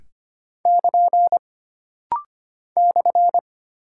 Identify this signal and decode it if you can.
Morse code, heard as a keyed tone, about 25 wpm, 700 Hz. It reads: <KN> /
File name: Single Punctuation-Prosign - Rapid-Fire - 25wpm